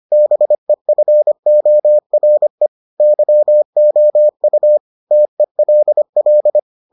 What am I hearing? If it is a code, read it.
Morse code, heard as a keyed tone, about 25 wpm, 600 Hz. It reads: BEFORE YOU TELL